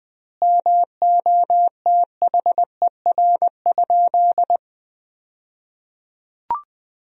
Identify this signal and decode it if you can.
Morse code, heard as a keyed tone, about 21 wpm, 700 Hz. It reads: MOTHER?
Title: 2x Word-Spacing - Sentences from Top 200 Words - 20wpm